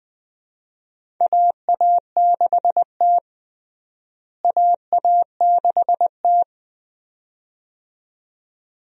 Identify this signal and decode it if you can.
Morse code, heard as a keyed tone, about 20 wpm, 700 Hz. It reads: AA6T AA6T
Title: USA Call Signs - 4 Characters - Repeated 2x - 20wpm